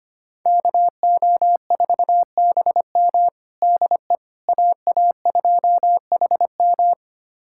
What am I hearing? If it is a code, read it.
Morse code, heard as a keyed tone, about 25 wpm, 700 Hz. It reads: KO4BM DE AA2HM
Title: USA Callsign DE Callsign - 5 Characters - 25wpm